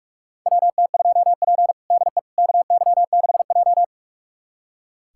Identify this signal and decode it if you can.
Morse code, heard as a keyed tone, about 46 wpm, 700 Hz. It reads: WT2P DE KY6J
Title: USA Callsign DE Callsign - 4 Characters - 45wpm